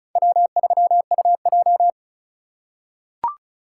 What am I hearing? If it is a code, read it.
Morse code, heard as a keyed tone, about 34 wpm, 700 Hz. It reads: W3UJ
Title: USA Call Signs - 4 Characters - 6x Character Spacing - 35wpm